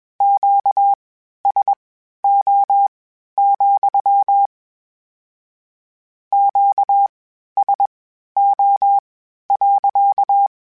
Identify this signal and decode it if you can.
Morse code, heard as a keyed tone, about 21 wpm, spaced as for 12 wpm, 800 Hz. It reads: QSO, QSO.
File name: Q Calls / Q Calls-mp3s / Q Calls sppc @20